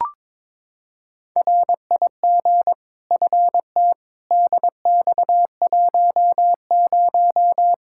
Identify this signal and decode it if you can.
Morse code, heard as a keyed tone, about 22 wpm, 700 Hz. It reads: RIG FT DX10
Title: Practice QSO - Exchange 2 - Info - 22wpm